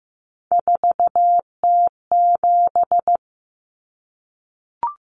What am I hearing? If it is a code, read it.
Morse code, heard as a keyed tone, about 15 wpm, 700 Hz. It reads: 4T7
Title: Common Cut Numbers - 3-Digit 15wpm